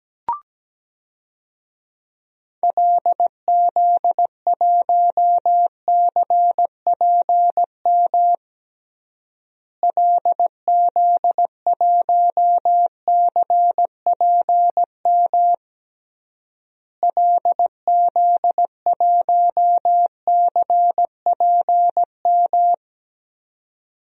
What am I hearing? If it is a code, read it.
Morse code, heard as a keyed tone, about 17 wpm, 700 Hz. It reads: LZ1CPM LZ1CPM LZ1CPM
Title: DXCC-100 Call Signs - 6 Characters - Repeated 3x - 17wpm